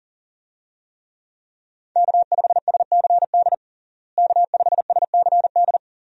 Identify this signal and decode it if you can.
Morse code, heard as a keyed tone, about 37 wpm, 700 Hz. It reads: K5SCD K5SCD
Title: USA Call Signs - 5 Characters - Repeated 2x - 40wpm